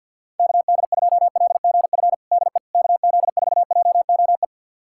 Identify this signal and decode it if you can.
Morse code, heard as a keyed tone, about 51 wpm, 700 Hz. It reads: KD2LGF DE KZ4JOE
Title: USA Callsign DE Callsign - 6 Characters - 50wpm